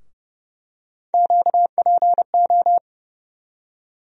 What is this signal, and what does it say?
Morse code, heard as a keyed tone, about 30 wpm, 700 Hz. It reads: QPO
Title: USA Call Sign Postfixes - 3 Letters 30wpm